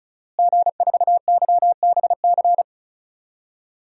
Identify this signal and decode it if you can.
Morse code, heard as a keyed tone, about 34 wpm, 700 Hz. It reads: G4YBC
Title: DXCC-100 Call Signs - 5 Characters - Repeated 2x - 35wpm